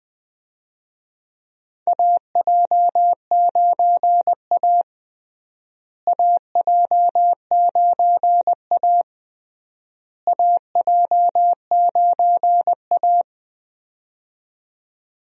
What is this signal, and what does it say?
Morse code, heard as a keyed tone, about 20 wpm, 700 Hz. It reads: AJ9A AJ9A AJ9A
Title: USA Call Signs - 4 Characters - Repeated 3x - 20wpm